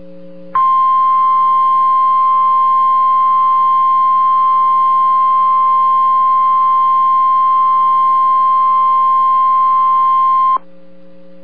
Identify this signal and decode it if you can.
Emergency Alert System (SAME) tones (the 1050 Hz attention tone). It is heard from 0.5 to 10.6 s.